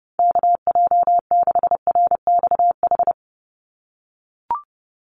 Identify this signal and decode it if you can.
Morse code, heard as a keyed tone, about 30 wpm, 700 Hz. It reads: KJ6RXH
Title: USA Call Signs 30wpm